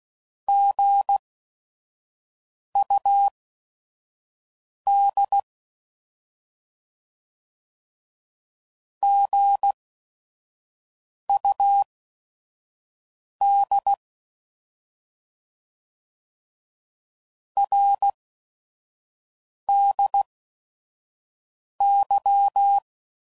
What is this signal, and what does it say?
Morse code, heard as a keyed tone, about 16 wpm, spaced as for 5 wpm, 800 Hz. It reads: GUD GUD RDY
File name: Mixed / Abreviations / Abreviations